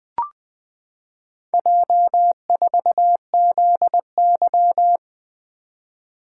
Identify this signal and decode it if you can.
Morse code, heard as a keyed tone, about 20 wpm, 700 Hz. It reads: J4ZY
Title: Four Letter-Number Combinations - 20wpm